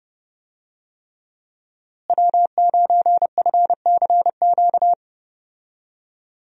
Morse code, heard as a keyed tone, about 30 wpm, 700 Hz. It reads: W9FCQ
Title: USA Call Signs - 5 Characters - Repeated 3x - 30wpm